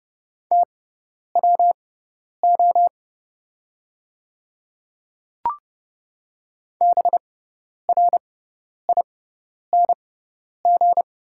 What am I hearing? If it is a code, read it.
Morse code, heard as a keyed tone, about 30 wpm, spaced as for 10 wpm, 690 Hz. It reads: TWO BRING
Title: 6x Character Spacing - Top 500 Words - 30wpm